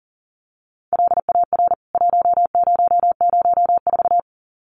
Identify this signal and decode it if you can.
Morse code, heard as a keyed tone, about 40 wpm, 700 Hz. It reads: LAR 1004
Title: CWT Exchange 40wpm